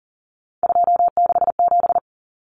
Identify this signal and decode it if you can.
Morse code, heard as a keyed tone, about 40 wpm, 700 Hz. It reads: 267
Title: Numbers - 3-Digit 40wpm